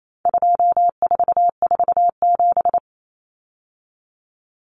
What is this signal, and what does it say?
Morse code, heard as a keyed tone, about 28 wpm, 700 Hz. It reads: 2447
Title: Numbers - 4-Digit 28wpm